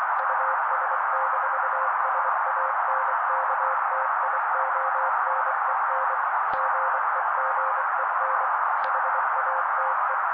Morse code, heard as a keyed tone, about 24 wpm, 530 Hz. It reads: US-SANKTIONEN GEGEN HATE
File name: daily / 23wpm noise